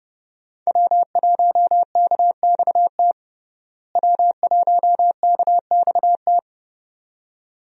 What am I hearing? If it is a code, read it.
Morse code, heard as a keyed tone, about 30 wpm, 700 Hz. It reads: W1KXT W1KXT
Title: USA Call Signs - 5 Characters - Repeated 2x - 30wpm